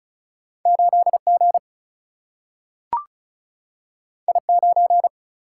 Morse code, heard as a keyed tone, about 36 wpm, 700 Hz. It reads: 8G I9
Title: Single-Letter Single-Number 35wpm